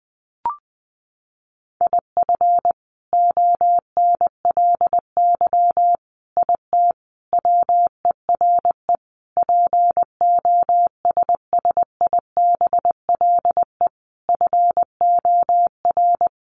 Morse code, heard as a keyed tone, about 20 wpm, 700 Hz. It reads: IF ONLY IT WERE POSSIBLE FOR
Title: Sentences from Top 500 Words 20wpm